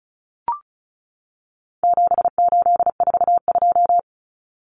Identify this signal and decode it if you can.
Morse code, heard as a keyed tone, about 34 wpm, 700 Hz. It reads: 7842
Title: Numbers - 4-Digit 35wpm